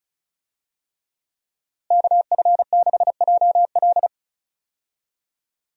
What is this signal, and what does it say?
Morse code, heard as a keyed tone, about 34 wpm, 700 Hz. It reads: KF6JL
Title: USA Call Signs - 5 Characters - Repeated 3x - 35wpm